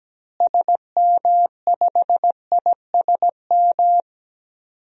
Morse code, heard as a keyed tone, about 17 wpm, 700 Hz. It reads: SM5ISM
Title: DXCC-100 Call Signs - 6 Characters - Repeated 3x - 17wpm